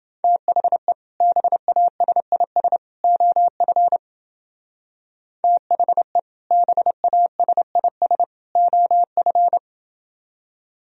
Morse code, heard as a keyed tone, about 29 wpm, 700 Hz. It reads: THE BASIS OF THE BASIS OF